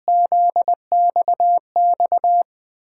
Morse code, heard as a keyed tone, about 20 wpm, 700 Hz. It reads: ZXX
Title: USA Call Signs - 6 Characters - 8x Character Spacing - 20wpm